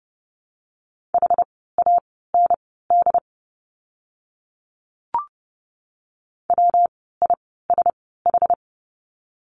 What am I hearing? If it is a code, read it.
Morse code, heard as a keyed tone, about 30 wpm, spaced as for 17 wpm, 700 Hz. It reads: HAND WISH
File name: course / 30:10wpm / 115 Lesson 19f - Letter w - Words 30wpm